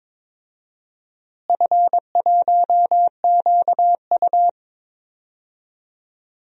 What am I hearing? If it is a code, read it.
Morse code, heard as a keyed tone, about 22 wpm, 700 Hz. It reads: F1QU
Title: DXCC-100 Call Signs - 4 Characters - Repeated 2x - 22wpm